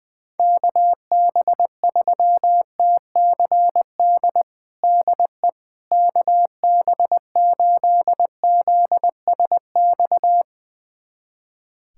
Morse code, heard as a keyed tone, about 20 wpm, 700 Hz. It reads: KB3TCD DE KB8ZSX
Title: USA Callsign DE Callsign - 6 Characters - 20wpm